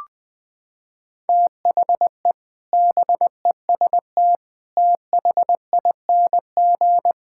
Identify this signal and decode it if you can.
Morse code, heard as a keyed tone, about 21 wpm, 700 Hz. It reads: THE BEST THING
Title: Sentences with 8-Word Phrases - 20wpm